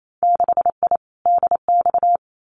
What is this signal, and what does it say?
Morse code, heard as a keyed tone, about 28 wpm, 700 Hz. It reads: 6I DX
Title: Winter Field Day Exchanges Encore 28wpm